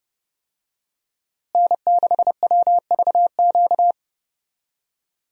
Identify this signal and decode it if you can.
Morse code, heard as a keyed tone, about 30 wpm, 700 Hz. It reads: N6WVQ
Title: USA Call Signs - 5 Characters - Repeated 3x - 30wpm